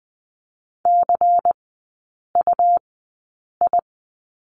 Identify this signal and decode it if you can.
Morse code, heard as a keyed tone, about 20 wpm, spaced as for 8 wpm, 700 Hz. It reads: CUI
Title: 152 Lesson 25e - Letter p - Three Characters 20wpm